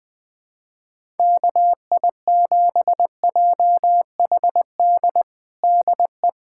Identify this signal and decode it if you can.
Morse code, heard as a keyed tone, about 20 wpm, 700 Hz. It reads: KI7JHD DE
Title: USA Callsign DE Callsign 20wpm